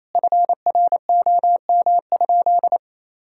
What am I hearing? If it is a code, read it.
Morse code, heard as a keyed tone, about 28 wpm, 700 Hz. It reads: FROM?
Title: Sentences from Top 1000 Words - Review Part 6 - 28wpm